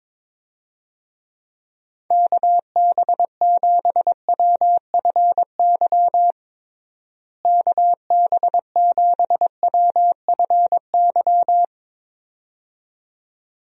Morse code, heard as a keyed tone, about 22 wpm, 700 Hz. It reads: KB7WFY KB7WFY
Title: USA Call Signs - 6 Characters - Repeated 2x - 22wpm